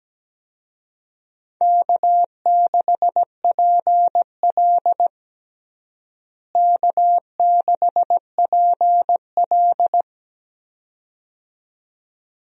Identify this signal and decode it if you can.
Morse code, heard as a keyed tone, about 17 wpm, 700 Hz. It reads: K6PL K6PL
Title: USA Call Signs - 4 Characters - Repeated 2x - 17wpm